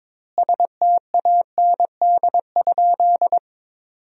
Morse code, heard as a keyed tone, about 22 wpm, 700 Hz. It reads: STAND?
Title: 2x Word-Spacing - Sentences from Top 300 Words - 22wpm